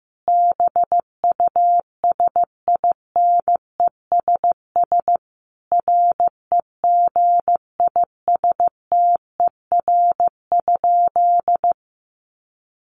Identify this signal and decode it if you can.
Morse code, heard as a keyed tone, about 15 wpm, 700 Hz. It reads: BUSINESS REGISTER?
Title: Sentences from Top 1200 Words 15wpm